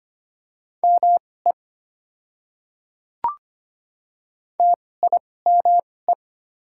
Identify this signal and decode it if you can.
Morse code, heard as a keyed tone, about 25 wpm, spaced as for 18 wpm, 700 Hz. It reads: ME TIME
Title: 2x Character Spacing - Top 500 Words - 25wpm